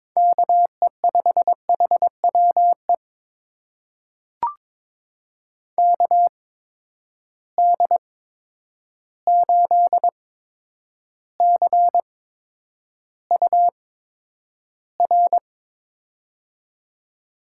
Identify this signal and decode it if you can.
Morse code, heard as a keyed tone, about 22 wpm, spaced as for 9 wpm, 700 Hz. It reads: KE5HWE KD8CUR
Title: USA Call Signs - 6 Characters - 8x Character Spacing - 22wpm